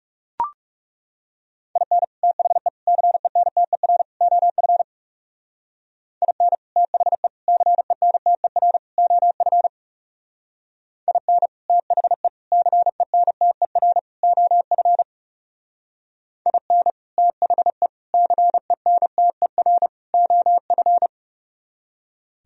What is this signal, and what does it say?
Morse code, heard as a keyed tone, about 38 wpm, 700 Hz. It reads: IN THE CENTER OF IN THE CENTER OF IN THE CENTER OF IN THE CENTER OF